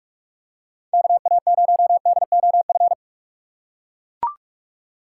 Morse code, heard as a keyed tone, about 42 wpm, 700 Hz. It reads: KA0DOF
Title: USA Call Signs - 6 Characters - 4x Character Spacing - 45wpm